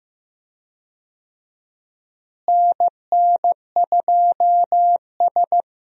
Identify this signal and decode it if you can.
Morse code, heard as a keyed tone, about 15 wpm, 700 Hz. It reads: NN2S
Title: USA Call Signs - 4 Characters - Repeated 3x - 15wpm